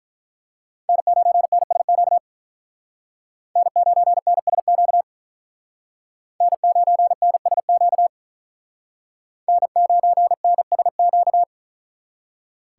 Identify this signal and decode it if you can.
Morse code, heard as a keyed tone, about 42 wpm, 700 Hz. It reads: N9NSQ N9NSQ N9NSQ N9NSQ
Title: Speed Racing - USA 1x3 Call Signs 35wpm